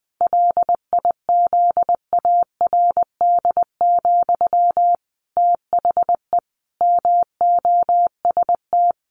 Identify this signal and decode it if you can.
Morse code, heard as a keyed tone, about 20 wpm, 700 Hz. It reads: LIZARD, THE MOST
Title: A Sound Of Thunder 20wpm